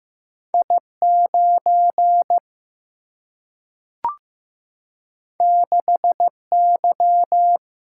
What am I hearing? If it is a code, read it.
Morse code, heard as a keyed tone, about 15 wpm, 700 Hz. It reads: I 9 6 Y
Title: Single-Letter Single-Number 15wpm